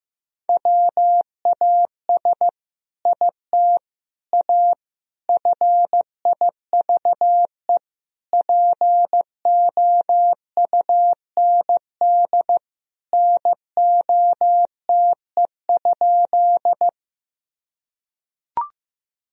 Morse code, heard as a keyed tone, about 15 wpm, 700 Hz. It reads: WAS IT A FIVE POUND NOTE?
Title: Sentences from Top 1000 Words - Review Part 2 - 15wpm